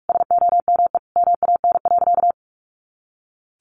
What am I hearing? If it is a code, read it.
Morse code, heard as a keyed tone, about 44 wpm, 700 Hz. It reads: SOME MAN.
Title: Sentences from Top 200 Words 45wpm